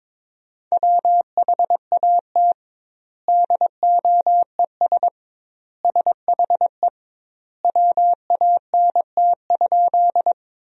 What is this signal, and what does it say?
Morse code, heard as a keyed tone, about 22 wpm, 700 Hz. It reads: WHAT DOES SHE WANT?
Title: 2x Word-Spacing - Sentences from Top 200 Words - 22wpm